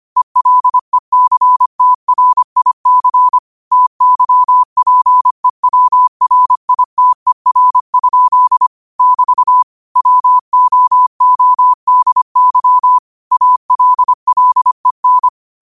Morse code, heard as a keyed tone, about 25 wpm, 1000 Hz. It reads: ELECTRIC TYPEWRITER? <BT> WOODY ALLEN
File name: qotdfiles / 25wpm